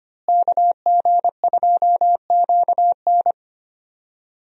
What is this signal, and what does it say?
Morse code, heard as a keyed tone, about 25 wpm, 700 Hz. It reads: KG2QN
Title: USA Call Signs - 5 Characters - Repeated 2x - 25wpm